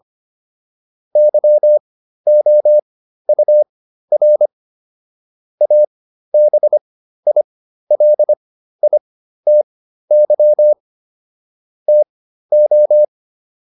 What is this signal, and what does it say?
Morse code, heard as a keyed tone, about 25 wpm, spaced as for 13 wpm, 600 Hz. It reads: YOUR ABILITY TO